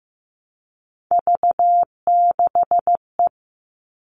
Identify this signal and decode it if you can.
Morse code, heard as a keyed tone, about 15 wpm, 700 Hz. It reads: V6E
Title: All Cut Numbers - 3-Digit 15wpm